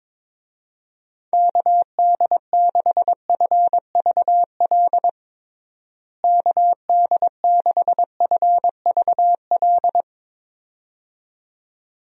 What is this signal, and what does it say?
Morse code, heard as a keyed tone, about 22 wpm, 700 Hz. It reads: KD6FVL KD6FVL
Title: USA Call Signs - 6 Characters - Repeated 2x - 22wpm